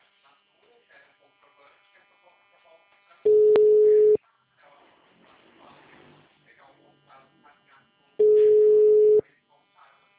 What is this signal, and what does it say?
Ringback tone (Europe, 425 Hz), with speech.
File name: out- -2010- - - .20706